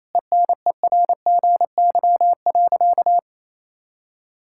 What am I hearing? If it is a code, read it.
Morse code, heard as a keyed tone, about 28 wpm, 700 Hz. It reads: ENERGY.
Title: Sentences from Top 1000 Words - Review Part 6 - 28wpm